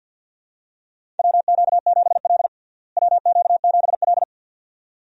Morse code, heard as a keyed tone, about 50 wpm, 700 Hz. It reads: WQ7L WQ7L
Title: USA Call Signs - 4 Characters - Repeated 2x - 50wpm